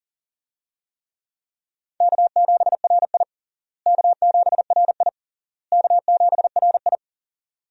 Morse code, heard as a keyed tone, about 40 wpm, 700 Hz. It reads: K7RI K7RI K7RI
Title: USA Call Signs - 4 Characters - Repeated 3x - 40wpm